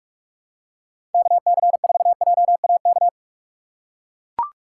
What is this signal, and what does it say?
Morse code, heard as a keyed tone, about 43 wpm, 700 Hz. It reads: KC4JAK
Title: USA Call Signs - 6 Characters - 3x Character Spacing - 45wpm